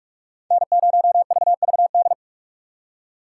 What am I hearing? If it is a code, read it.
Morse code, heard as a keyed tone, about 59 wpm, 700 Hz. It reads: N0VVD